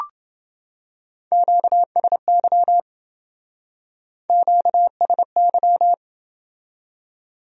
Morse code, heard as a keyed tone, about 29 wpm, 700 Hz. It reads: QSY QSY